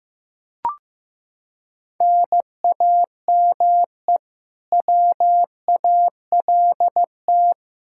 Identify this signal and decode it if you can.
Morse code, heard as a keyed tone, about 15 wpm, 700 Hz. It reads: NAME WALT
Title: Practice QSO - Exchange 1 - Beginning - 15wpm